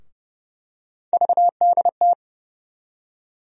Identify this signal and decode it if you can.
Morse code, heard as a keyed tone, about 30 wpm, 700 Hz. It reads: VDT
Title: USA Call Sign Postfixes - 3 Letters 30wpm